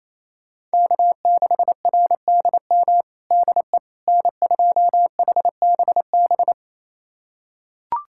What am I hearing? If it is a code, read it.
Morse code, heard as a keyed tone, about 28 wpm, 700 Hz. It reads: K6RDM DE N2HBB
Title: USA Callsign DE Callsign - 5 Characters - 28wpm